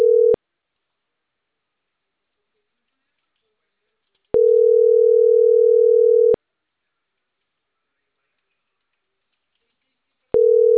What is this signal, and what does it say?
Ringback tone (North America), 440 + 480 Hz.